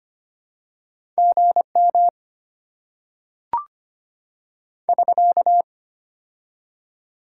Morse code, heard as a keyed tone, about 25 wpm, 700 Hz. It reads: GM <SK>
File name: Top 90 QSO Elements - 6x Character Spacing - 25wpm